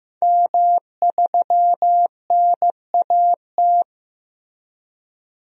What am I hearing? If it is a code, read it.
Morse code, heard as a keyed tone, about 15 wpm, 700 Hz. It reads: M3NAT
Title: DXCC-100 Call Signs - 5 Characters - Repeated 2x - 15wpm